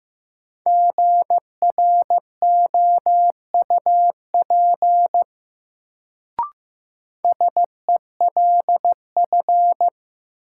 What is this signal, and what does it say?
Morse code, heard as a keyed tone, about 15 wpm, 700 Hz. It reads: GROUP SELF
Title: Sentences from Top 1000 Words - Review Part 1 - New Word Drill - 15wpm